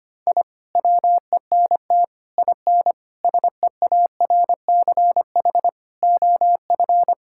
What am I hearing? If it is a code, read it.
Morse code, heard as a keyed tone, about 25 wpm, 700 Hz. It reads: I WENT IN SEARCH OF
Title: Sentences from Top 1000 Words - Review Part 5 - 25wpm